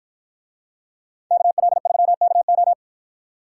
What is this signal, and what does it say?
Morse code, heard as a keyed tone, about 49 wpm, 700 Hz. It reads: KL3KO